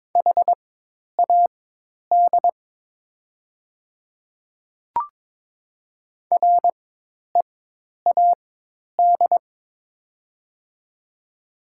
Morse code, heard as a keyed tone, about 22 wpm, spaced as for 10 wpm, 700 Hz. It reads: HAD READ
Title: 4x Character Spacing - Top 200 Words - 22wpm